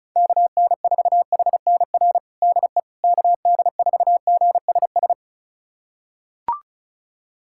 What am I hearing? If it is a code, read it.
Morse code, heard as a keyed tone, about 34 wpm, 700 Hz. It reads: KN4HNR DE KD4GSS
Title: USA Callsign DE Callsign - 6 Characters - 35wpm